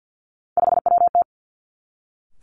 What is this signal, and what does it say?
Morse code, heard as a keyed tone, about 50 wpm, 700 Hz. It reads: 5WT
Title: Three Challenging Letter-Number Combinations 50wpm